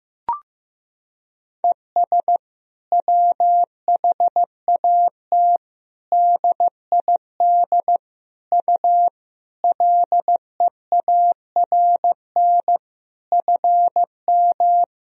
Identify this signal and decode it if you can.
Morse code, heard as a keyed tone, about 15 wpm, 700 Hz. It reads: ES WHAT DID U LEARN FM
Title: Sentences from Top 300 Words and Abbreviations 15wpm